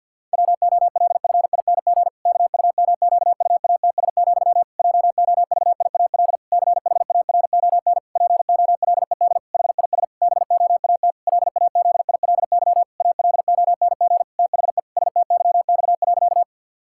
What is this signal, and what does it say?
Morse code, heard as a keyed tone, about 50 wpm, 700 Hz. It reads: WOLFING KUMQUATS, JOVIAL CHARON POLED HIS BOAT LAZILY ALONG THE STYX.
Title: Pangram Challenge v2 - 50wpm